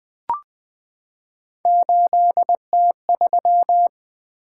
Morse code, heard as a keyed tone, about 20 wpm, 700 Hz. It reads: 8T3
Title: Three Letter-Number Combinations - 20wpm